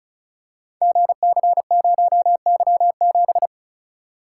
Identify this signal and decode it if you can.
Morse code, heard as a keyed tone, about 35 wpm, 700 Hz. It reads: GC0Y7
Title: Code Groups - 5 Letter:Numbers - 35wpm